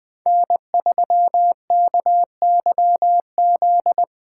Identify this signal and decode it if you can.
Morse code, heard as a keyed tone, about 20 wpm, 700 Hz. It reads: N3KYZ
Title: USA Call Signs - 5 Characters - 3x Character Spacing - 20wpm